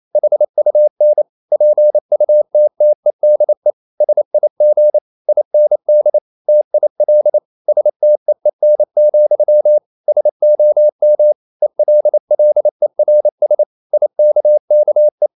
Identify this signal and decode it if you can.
Morse code, heard as a keyed tone, about 28 wpm, 600 Hz. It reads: HUN PUTTEDE SIG IND TIL STEEN, SOM ELLERS IKKE
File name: Ferie QRQ 28wpm